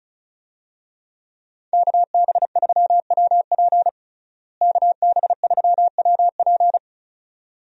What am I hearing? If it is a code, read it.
Morse code, heard as a keyed tone, about 36 wpm, 700 Hz. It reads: KB3WP KB3WP
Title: USA Call Signs - 5 Characters - Repeated 2x - 35wpm